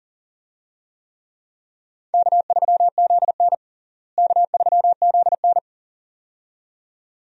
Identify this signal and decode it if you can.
Morse code, heard as a keyed tone, about 40 wpm, 700 Hz. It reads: K3ZN K3ZN
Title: USA Call Signs - 4 Characters - Repeated 2x - 40wpm